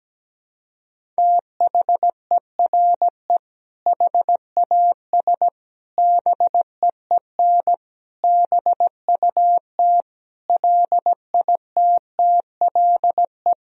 Morse code, heard as a keyed tone, about 17 wpm, 700 Hz. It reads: THERE HAS BEEN BUT LITTLE
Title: Sentences from Top 1000 Words - Review Part 1 - 17wpm